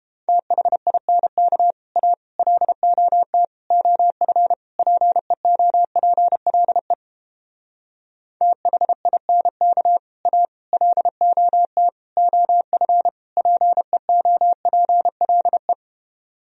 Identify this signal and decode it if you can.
Morse code, heard as a keyed tone, about 31 wpm, 700 Hz. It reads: THINK A LOT OF PEOPLE THINK A LOT OF PEOPLE